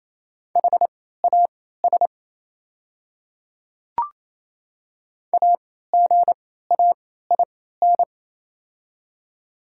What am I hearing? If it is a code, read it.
Morse code, heard as a keyed tone, about 28 wpm, spaced as for 16 wpm, 700 Hz. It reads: HAS AGAIN
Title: 3x Character Spacing - Top 200 Words - 28wpm